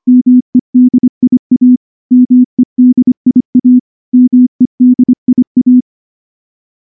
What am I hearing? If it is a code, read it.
Morse code, heard as a keyed tone, about 24 wpm, 250 Hz. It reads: MEDIA MEDIA MEDIA
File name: morse / morse-training-25WPM